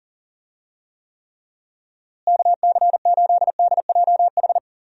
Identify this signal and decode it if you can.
Morse code, heard as a keyed tone, about 40 wpm, 700 Hz. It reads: KC8DJH
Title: USA Call Signs - 6 Characters - Repeated 3x - 40wpm